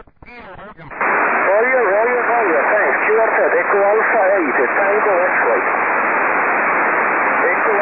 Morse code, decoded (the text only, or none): none